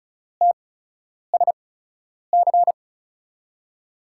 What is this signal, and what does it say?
Morse code, heard as a keyed tone, about 36 wpm, spaced as for 10 wpm, 700 Hz. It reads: TSC
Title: USA Call Signs - 6 Characters - 8x Character Spacing - 35wpm